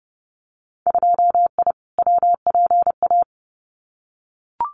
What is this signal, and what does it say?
Morse code, heard as a keyed tone, about 30 wpm, 700 Hz. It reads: 2I WPA
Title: Winter Field Day Exchanges 30wpm